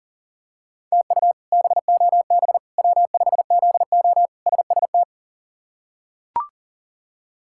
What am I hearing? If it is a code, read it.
Morse code, heard as a keyed tone, about 40 wpm, 700 Hz. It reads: TU BOB W5ZO SST
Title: K1USN SST QSO Practice - 40wpm